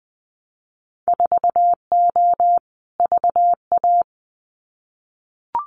Morse code, heard as a keyed tone, about 19 wpm, 700 Hz. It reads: 4O VA
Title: Winter Field Day Exchanges Encore 20wpm